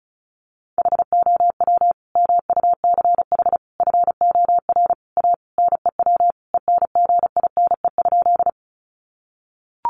Morse code, heard as a keyed tone, about 34 wpm, 700 Hz. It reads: HOW MUCH FOR A NEW ENGINE?
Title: Sentences from Top 1500 Words 35wpm